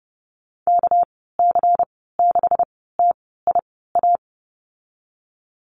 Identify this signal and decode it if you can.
Morse code, heard as a keyed tone, about 30 wpm, spaced as for 17 wpm, 700 Hz. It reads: KC6TIA
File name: Lesson 37g - Letter x - Call Signs 30wpm